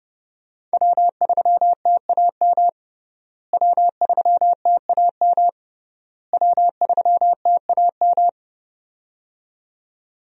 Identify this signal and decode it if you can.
Morse code, heard as a keyed tone, about 32 wpm, 700 Hz. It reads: W3TAM W3TAM W3TAM
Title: USA Call Signs - 5 Characters - Repeated 3x - 30wpm